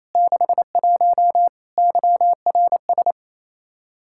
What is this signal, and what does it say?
Morse code, heard as a keyed tone, about 27 wpm, 700 Hz. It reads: 61 YRS
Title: K1USN SST QSO Practice 28wpm